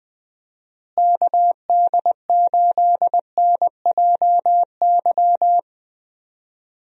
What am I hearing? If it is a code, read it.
Morse code, heard as a keyed tone, about 20 wpm, 700 Hz. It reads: KD8NJY
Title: USA Call Signs - 6 Characters - Repeated 2x - 20wpm